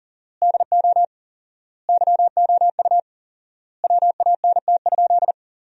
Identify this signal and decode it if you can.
Morse code, heard as a keyed tone, about 37 wpm, 700 Hz. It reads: DO YOU WANT?
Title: 4x Word-Spacing - Sentences from Top 200 Words - 40wpm